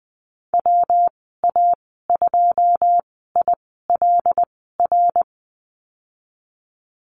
Morse code, heard as a keyed tone, about 20 wpm, spaced as for 15 wpm, 700 Hz. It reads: WA2ILR
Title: 161 Lesson 26g - Letter g - Call Signs 20wpm